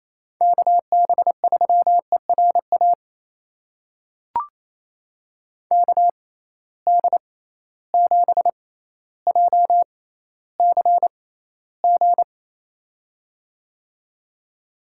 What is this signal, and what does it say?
Morse code, heard as a keyed tone, about 28 wpm, spaced as for 14 wpm, 700 Hz. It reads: KB3ERA KD7JCG
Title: USA Call Signs - 6x Character Spacing - 28wpm